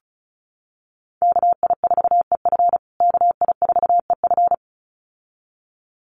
Morse code, heard as a keyed tone, about 34 wpm, 700 Hz. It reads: KI4EF KI4EF
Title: USA Call Signs Repeated 2x 35wpm